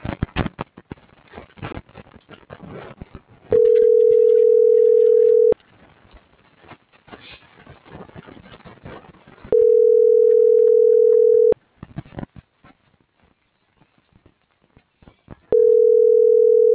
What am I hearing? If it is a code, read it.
Ringback tone (North America), 440 + 480 Hz.